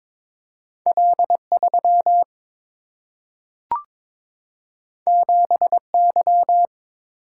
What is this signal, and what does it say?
Morse code, heard as a keyed tone, about 22 wpm, 700 Hz. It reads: L3 7Y
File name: practice / 22wpm / Two Letter-Number Combinations - 22wpm